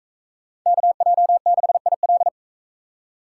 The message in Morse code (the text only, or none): KJ6IL